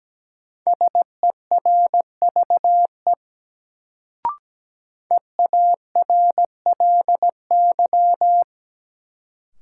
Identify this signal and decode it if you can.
Morse code, heard as a keyed tone, about 17 wpm, 700 Hz. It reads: SERVE EARLY
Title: Sentences from Top 400 Words - New Word Drill - 17wpm